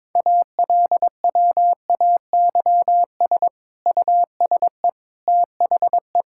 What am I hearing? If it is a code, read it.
Morse code, heard as a keyed tone, about 22 wpm, 700 Hz. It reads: ALWAYS USE THE
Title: Sentences from Top 1000 Words - Review Part 6 - 22wpm